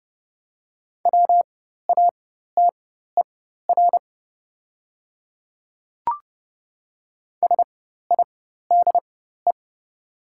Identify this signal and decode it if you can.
Morse code, heard as a keyed tone, about 30 wpm, spaced as for 14 wpm, 700 Hz. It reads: WATER SIDE
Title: 4x Character Spacing - Top 100 Words - 30wpm